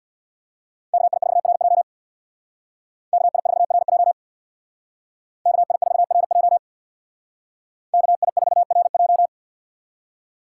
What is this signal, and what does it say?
Morse code, heard as a keyed tone, about 64 wpm, 700 Hz. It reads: KI4RJ KI4RJ KI4RJ KI4RJ